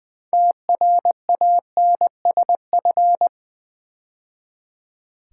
Morse code, heard as a keyed tone, about 20 wpm, 700 Hz. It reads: TRANSF
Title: Build A Word - 2 Letters At A Time - 8-Letter Words - 20wpm